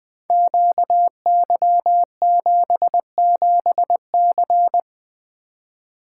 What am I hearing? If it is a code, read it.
Morse code, heard as a keyed tone, about 20 wpm, 700 Hz. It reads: QY77C
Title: Code Groups - 5 Letter:Numbers - 20wpm